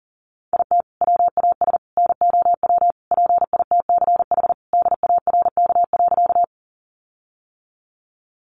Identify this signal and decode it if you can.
Morse code, heard as a keyed tone, about 40 wpm, 700 Hz. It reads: IT WAS NOW PITCH DARK.
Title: Sentences from Top 1300 Words 40wpm